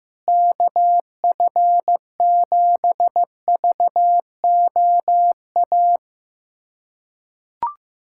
Morse code, heard as a keyed tone, about 15 wpm, 700 Hz. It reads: KF7VOA
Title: USA Callsign DE Callsign - 6 Characters - 15wpm